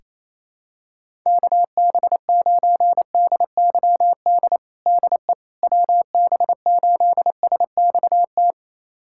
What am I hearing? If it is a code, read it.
Morse code, heard as a keyed tone, about 28 wpm, 700 Hz. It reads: KB9DYD DE WB8SXT
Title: USA Callsign DE Callsign - 6 Characters - 28wpm